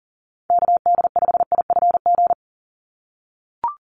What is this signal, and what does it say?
Morse code, heard as a keyed tone, about 40 wpm, 690 Hz. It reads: KD5IFG
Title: USA Call Signs 40wpm